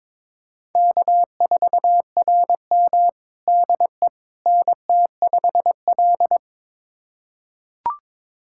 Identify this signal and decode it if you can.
Morse code, heard as a keyed tone, about 22 wpm, 700 Hz. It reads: K4RM DE NT5L
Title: USA Callsign DE Callsign - 4 Characters - 22wpm